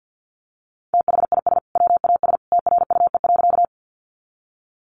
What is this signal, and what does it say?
Morse code, heard as a keyed tone, about 50 wpm, 700 Hz. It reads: THIS WAS TRUE.
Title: Sentences from Top 400 Words 50wpm